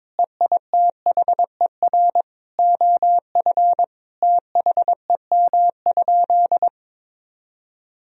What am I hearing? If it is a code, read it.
Morse code, heard as a keyed tone, about 22 wpm, 700 Hz. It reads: EITHER OF THEM?
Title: Sentences from Top 1000 Words - Review Part 4 - 22wpm